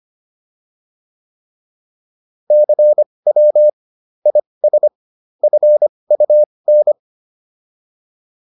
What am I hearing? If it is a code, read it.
Morse code, heard as a keyed tone, about 25 wpm, spaced as for 20 wpm, 600 Hz. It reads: CW IS FUN